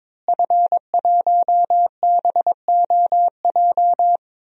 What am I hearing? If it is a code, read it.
Morse code, heard as a keyed tone, about 22 wpm, 700 Hz. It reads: F1BOJ
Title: DXCC-100 Call Signs - 5 Characters - Repeated 3x - 22wpm